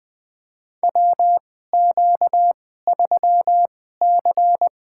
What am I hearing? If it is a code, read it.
Morse code, heard as a keyed tone, about 20 wpm, spaced as for 15 wpm, 700 Hz. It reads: WQ3C
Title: USA Call Signs - 4 Characters - 2x Character Spacing - 20wpm